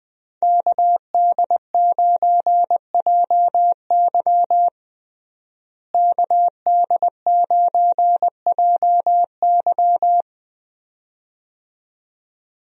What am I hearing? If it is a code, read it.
Morse code, heard as a keyed tone, about 20 wpm, 700 Hz. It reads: KD9JY KD9JY
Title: USA Call Signs - 5 Characters - Repeated 2x - 20wpm